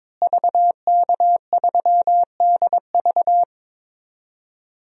Morse code, heard as a keyed tone, about 22 wpm, 700 Hz. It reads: VK3DV
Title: DXCC-100 Call Signs - 5 Characters - Repeated 2x - 22wpm